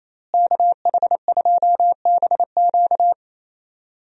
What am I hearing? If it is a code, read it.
Morse code, heard as a keyed tone, about 28 wpm, 700 Hz. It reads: KH2BQ
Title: USA Call Signs - 5 Characters - Repeated 2x - 28wpm